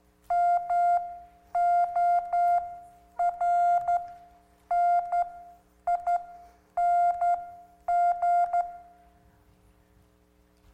Morse code, heard as a keyed tone, about 12 wpm, spaced as for 9 wpm, 700 Hz. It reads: MORNING